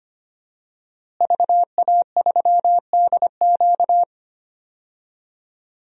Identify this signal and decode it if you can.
Morse code, heard as a keyed tone, about 25 wpm, 700 Hz. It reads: VA3DQ
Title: DXCC-100 Call Signs - 5 Characters - Repeated 3x - 25wpm